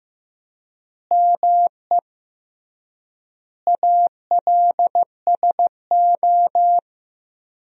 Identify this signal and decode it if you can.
Morse code, heard as a keyed tone, about 15 wpm, 700 Hz. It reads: ME ALSO
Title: 3x Word-Spacing - Sentences from Top 400 Words - 15wpm